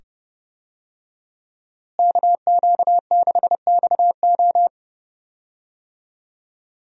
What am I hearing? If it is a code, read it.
Morse code, heard as a keyed tone, about 30 wpm, 700 Hz. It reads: KQ6XO
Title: USA Call Signs - 5 Characters - Repeated 2x - 30wpm